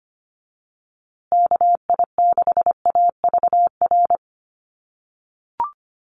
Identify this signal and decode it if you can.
Morse code, heard as a keyed tone, about 25 wpm, 700 Hz. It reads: KI6AVR
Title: USA Call Signs Repeated 2x 25wpm